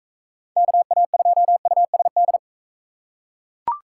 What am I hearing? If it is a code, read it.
Morse code, heard as a keyed tone, about 43 wpm, 700 Hz. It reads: KA2USD